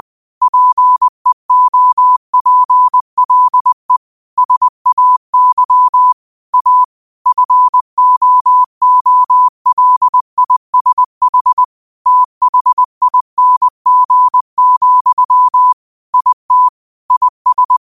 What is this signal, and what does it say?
Morse code, heard as a keyed tone, about 20 wpm, 1000 Hz. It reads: PEOPLE SAY A FOOLISH THING, IT IS
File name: qotdfiles / 20wpm